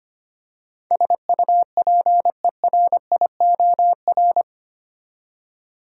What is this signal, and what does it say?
Morse code, heard as a keyed tone, about 25 wpm, 700 Hz. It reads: SUPERIOR
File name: Eight Letter Words - 25wpm